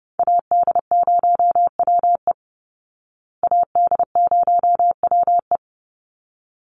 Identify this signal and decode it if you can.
Morse code, heard as a keyed tone, about 30 wpm, 700 Hz. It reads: AD0WE AD0WE
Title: Call Signs Repeated 3x - 30wpm